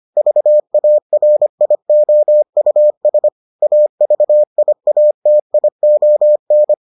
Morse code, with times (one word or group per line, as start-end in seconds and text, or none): 0.2-3.3 VARIOUS
3.6-6.7 AVIATION